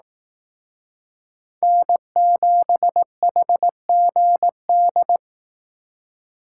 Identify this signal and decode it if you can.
Morse code, heard as a keyed tone, about 18 wpm, 700 Hz. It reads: N7HGD
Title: Speed Racing - USA 1x3 Call Signs 15wpm